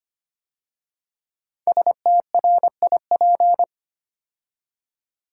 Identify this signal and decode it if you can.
Morse code, heard as a keyed tone, about 25 wpm, 700 Hz. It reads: STRIP
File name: 5 Letter Words - First 3 Letters Given - 25wpm